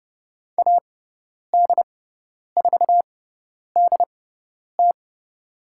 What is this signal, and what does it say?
Morse code, heard as a keyed tone, about 30 wpm, spaced as for 10 wpm, 700 Hz. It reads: AD4DT
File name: Lesson 18g - Letter m - Call Signs 30wpm